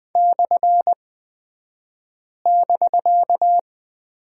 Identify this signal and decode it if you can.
Morse code, heard as a keyed tone, about 20 wpm, spaced as for 5 wpm, 700 Hz. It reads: /<BK>
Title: 249 Lesson 40d - BK - Two Characters 20wpm